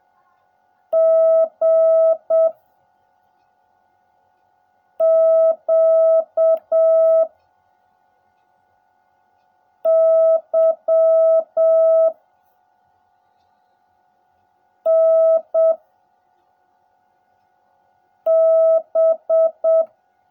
Morse code, heard as a keyed tone, about 7 wpm, 650 Hz. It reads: G Q Y N B
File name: jh8cbh / 2021cw / cw16a